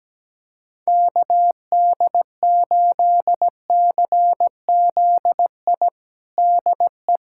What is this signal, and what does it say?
Morse code, heard as a keyed tone, about 17 wpm, 700 Hz. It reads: KD8CZI DE
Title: USA Callsign DE Callsign 17wpm